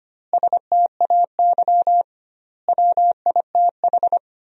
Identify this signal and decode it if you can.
Morse code, heard as a keyed tone, about 25 wpm, 700 Hz. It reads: STAY WITH
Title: 2x Word-Spacing - Sentences from Top 500 Words - 25wpm